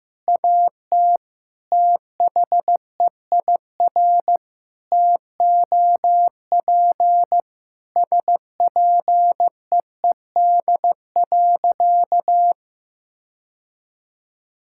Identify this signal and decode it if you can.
Morse code, heard as a keyed tone, about 16 wpm, 700 Hz. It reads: AT THEIR TOP SPEED.
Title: Sentences from Top 1000 Words - Review Part 5 - 15wpm